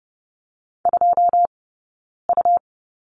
Morse code, heard as a keyed tone, about 30 wpm, spaced as for 9 wpm, 690 Hz. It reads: 2U
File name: Lesson 17d - Letter u - Two Characters 30wpm